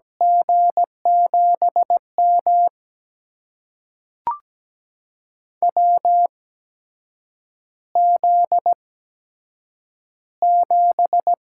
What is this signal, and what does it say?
Morse code, heard as a keyed tone, about 17 wpm, spaced as for 7 wpm, 700 Hz. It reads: G7M WZ7
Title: USA Call Signs - 4 Characters - 8x Character Spacing - 17wpm